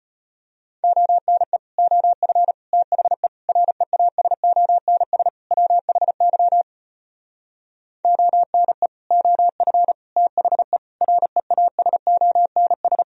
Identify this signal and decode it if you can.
Morse code, heard as a keyed tone, about 33 wpm, 700 Hz. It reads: ONE OF THE REASONS WHY ONE OF THE REASONS